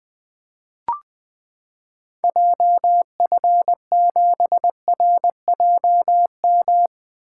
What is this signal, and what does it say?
Morse code, heard as a keyed tone, about 20 wpm, 700 Hz. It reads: JF7RJM
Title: 2023 POTA Activator Call Signs - 20wpm